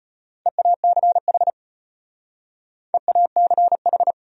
Morse code, heard as a keyed tone, about 32 wpm, 700 Hz. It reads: EACH EACH